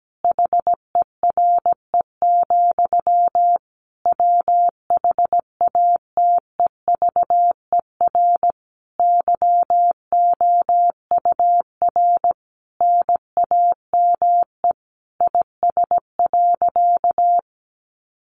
Morse code, heard as a keyed tone, about 17 wpm, 700 Hz. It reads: HERE, WHATEVER YOUR NAME IS.
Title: Sentences from Top 1100 Words 17wpm